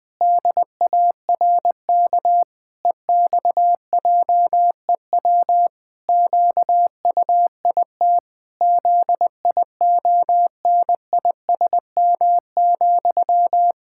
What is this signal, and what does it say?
Morse code, heard as a keyed tone, about 20 wpm, 700 Hz. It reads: DARK EXJEW QUIT ZIONISM,
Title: Pangram Challenge v2 - 20wpm